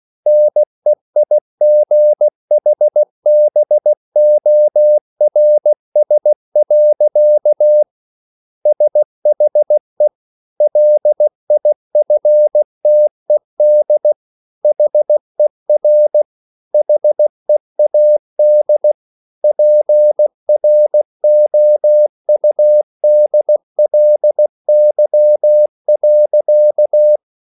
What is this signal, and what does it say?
Morse code, heard as a keyed tone, about 16 wpm, 600 Hz. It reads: NEIGHBORS. SHE LIFTED HER HEAD PROUDLY.